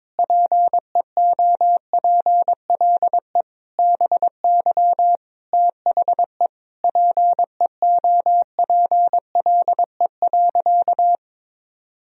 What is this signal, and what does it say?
Morse code, heard as a keyed tone, about 22 wpm, 700 Hz. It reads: PEOPLE BY THE PEOPLE.
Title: Sentences from Top 1000 Words - Review Part 3 - 22wpm